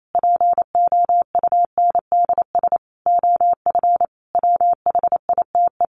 Morse code, heard as a keyed tone, about 28 wpm, 700 Hz. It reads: POUNDS OF WHITE
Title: A Sound Of Thunder 28wpm